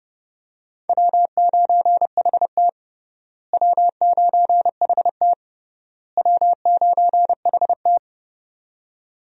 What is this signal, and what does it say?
Morse code, heard as a keyed tone, about 32 wpm, 700 Hz. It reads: W9HT W9HT W9HT
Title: USA Call Signs - 4 Characters - Repeated 3x - 30wpm